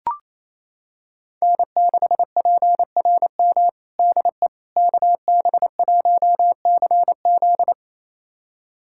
Morse code, heard as a keyed tone, about 28 wpm, 700 Hz. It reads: N6PRM DE KB1CZ
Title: USA Callsign DE Callsign - 5 Characters - 28wpm